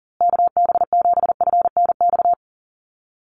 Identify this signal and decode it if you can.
Morse code, heard as a keyed tone, about 40 wpm, 700 Hz. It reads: KB7FNX
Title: USA Call Signs Repeated 2x 40wpm